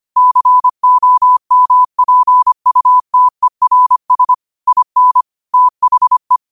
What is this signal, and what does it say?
Morse code, heard as a keyed tone, about 25 wpm, 1000 Hz. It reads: COMPUTERS IN THE
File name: qotdfiles / 25wpm